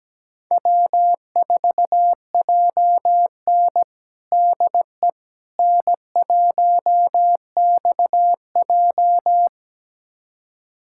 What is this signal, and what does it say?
Morse code, heard as a keyed tone, about 17 wpm, 700 Hz. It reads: W4JN DE N1XJ
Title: USA Callsign DE Callsign - 4 Characters - 17wpm